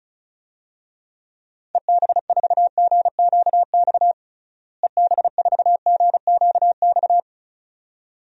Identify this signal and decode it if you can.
Morse code, heard as a keyed tone, about 36 wpm, 700 Hz. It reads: EB4GQX EB4GQX
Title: DXCC-100 Call Signs - 6 Characters - Repeated 2x - 35wpm